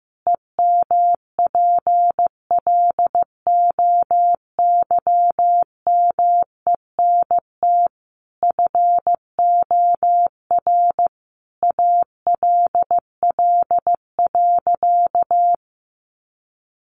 Morse code, heard as a keyed tone, about 15 wpm, 700 Hz. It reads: EMPLOYMENT FOR ALL.
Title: Sentences from Top 1500 Words 15wpm